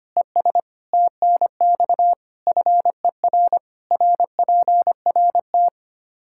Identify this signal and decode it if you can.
Morse code, heard as a keyed tone, about 25 wpm, 700 Hz. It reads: ES TNX FER RPRT
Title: Practice QSO - Exchange 1 - Beginning - 25wpm